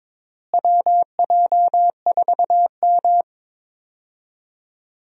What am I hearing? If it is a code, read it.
Morse code, heard as a keyed tone, about 22 wpm, 700 Hz. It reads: WJ4M
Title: USA Call Signs - 4 Characters - Repeated 3x - 22wpm